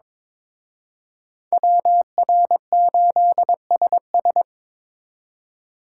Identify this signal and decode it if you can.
Morse code, heard as a keyed tone, about 26 wpm, 700 Hz. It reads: WR8SS
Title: Speed Racing - USA 2x2 Call Signs 22wpm